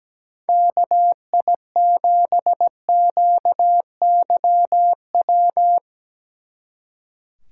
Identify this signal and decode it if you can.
Morse code, heard as a keyed tone, about 17 wpm, 700 Hz. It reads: KI7QYW
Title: Speed Racing - USA 2x3 Call Signs 17wpm